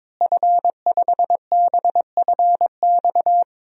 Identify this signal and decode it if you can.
Morse code, heard as a keyed tone, about 22 wpm, 700 Hz. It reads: F5BFX
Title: DXCC-100 Call Signs - 5 Characters - Repeated 3x - 22wpm